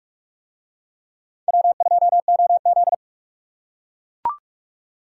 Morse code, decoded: W2OZ

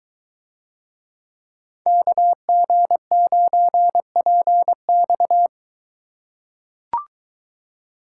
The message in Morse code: KG9PX

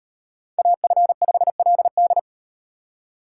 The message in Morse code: AF5LD

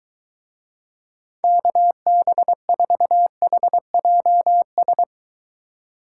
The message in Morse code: KB4HJS

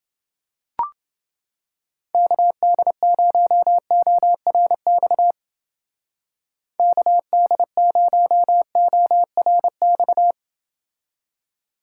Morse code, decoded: KD0ORX KD0ORX

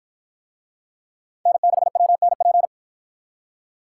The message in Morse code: N6WNP

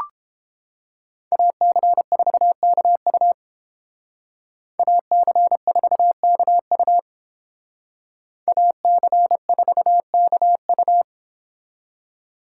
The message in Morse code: AC4KU AC4KU AC4KU

